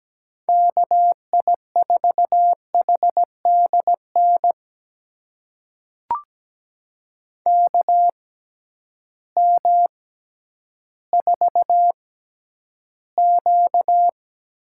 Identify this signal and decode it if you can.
Morse code, heard as a keyed tone, about 17 wpm, spaced as for 10 wpm, 700 Hz. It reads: KI4HDN KM4Q
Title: USA Call Signs - 6 Characters - 6x Character Spacing - 17wpm